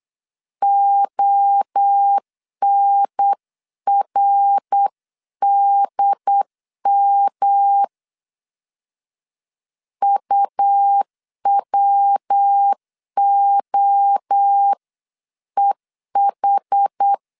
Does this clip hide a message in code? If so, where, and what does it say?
Yes: Morse code from 0.6 to 17.1 s, heard as a keyed tone, about 8 wpm, 800 Hz. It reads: ONRDM UWOEH